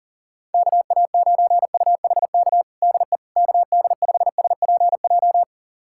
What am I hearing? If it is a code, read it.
Morse code, heard as a keyed tone, about 40 wpm, 700 Hz. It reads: KA9UHK DE KD5SPJ
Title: USA Callsign DE Callsign - 6 Characters - 40wpm